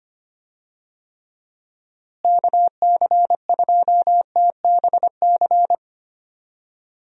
Morse code, heard as a keyed tone, about 25 wpm, 700 Hz. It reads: KC2TBC
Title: USA Call Signs - 6 Characters - Repeated 3x - 25wpm